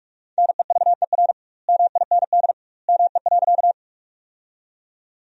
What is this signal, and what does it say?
Morse code, heard as a keyed tone, about 41 wpm, 700 Hz. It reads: NEVER MIND ME.
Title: 2x Word-Spacing - Sentences from Top 500 Words - 45wpm